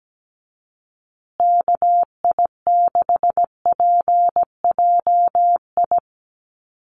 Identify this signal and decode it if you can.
Morse code, heard as a keyed tone, about 17 wpm, 700 Hz. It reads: KI6PJI
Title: USA Call Signs Repeated 3x 17wpm